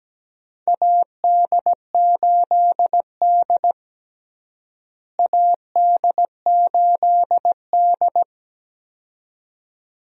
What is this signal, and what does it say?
Morse code, heard as a keyed tone, about 17 wpm, 700 Hz. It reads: AD8D AD8D
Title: USA Call Signs - 4 Characters - Repeated 2x - 17wpm